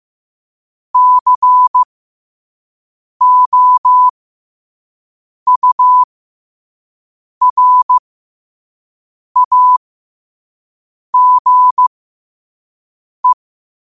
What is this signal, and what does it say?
Morse code, heard as a keyed tone, about 15 wpm, spaced as for 5 wpm, 1000 Hz. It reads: COURAGE
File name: qotdfiles / 5wpm